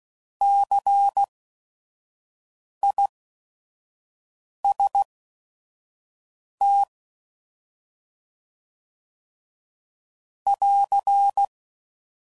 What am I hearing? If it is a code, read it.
Morse code, heard as a keyed tone, about 16 wpm, spaced as for 5 wpm, 800 Hz. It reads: CIST <AR>
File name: Prefixes / By First Letter / Cs